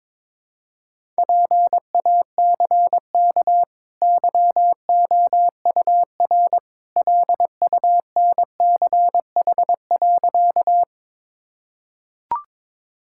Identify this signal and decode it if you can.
Morse code, heard as a keyed tone, about 22 wpm, 690 Hz. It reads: PACK YOUR LUNCH.
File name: practice / 22wpm / Easy Sentences with Hints - 22wpm